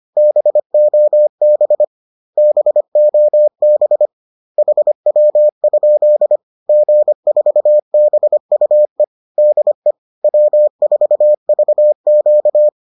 Morse code, heard as a keyed tone, about 25 wpm, 600 Hz. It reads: BOB BOB HW? G4BUE DE W4VQ